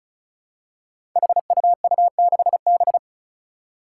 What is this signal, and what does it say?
Morse code, heard as a keyed tone, about 36 wpm, 700 Hz. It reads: HUU6B
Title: Code Groups - 5 Letter:Numbers - 35wpm